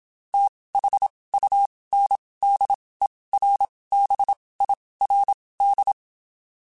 Morse code, heard as a keyed tone, about 27 wpm, spaced as for 19 wpm, 800 Hz. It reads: THUNDERBIRD
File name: QSOs / QSOs-mp3s / WD0BGP 1@25